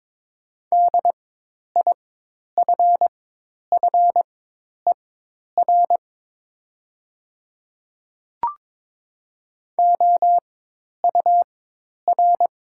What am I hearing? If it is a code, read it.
Morse code, heard as a keyed tone, about 22 wpm, spaced as for 10 wpm, 700 Hz. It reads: DIFFER OUR
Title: 4x Character Spacing - Top 300 Words - 22wpm